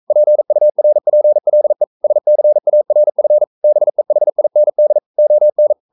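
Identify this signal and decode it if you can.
Morse code, heard as a keyed tone, about 42 wpm, 600 Hz. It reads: PURPLE SCARF BEHIND ON
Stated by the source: Morse code practice at 41 wpm, from CW med Gnister.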